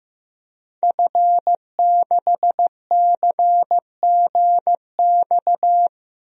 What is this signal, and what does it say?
Morse code, heard as a keyed tone, about 16 wpm, 700 Hz. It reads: F6CGX
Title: DXCC-100 Call Signs - 5 Characters - Repeated 3x - 15wpm